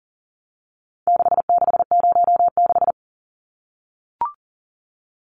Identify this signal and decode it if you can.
Morse code, heard as a keyed tone, about 40 wpm, 700 Hz. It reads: 6606
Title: Numbers - 4-Digit 40wpm